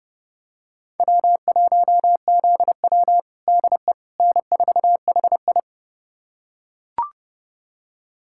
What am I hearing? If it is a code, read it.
Morse code, heard as a keyed tone, about 30 wpm, 700 Hz. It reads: W1ZW DE N4HI
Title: USA Callsign DE Callsign - 4 Characters - 30wpm